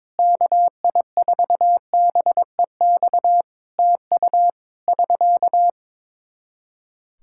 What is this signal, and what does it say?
Morse code, heard as a keyed tone, about 22 wpm, 700 Hz. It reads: KI4BEX TU <SK>
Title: K1USN SST QSO Practice 22wpm